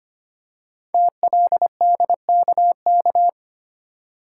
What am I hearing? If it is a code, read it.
Morse code, heard as a keyed tone, about 25 wpm, 700 Hz. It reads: TLDKK
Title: Code Groups - 5 Letters - 25wpm